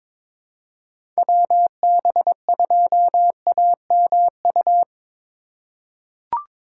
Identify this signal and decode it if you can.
Morse code, heard as a keyed tone, about 22 wpm, 700 Hz. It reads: WB2AMU
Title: USA Call Signs - 6 Characters - 2x Character Spacing - 22wpm